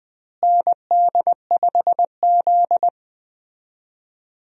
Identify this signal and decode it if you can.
Morse code, heard as a keyed tone, about 20 wpm, 700 Hz. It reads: ND5Z
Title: USA Call Signs - 4 Characters - Repeated 2x - 20wpm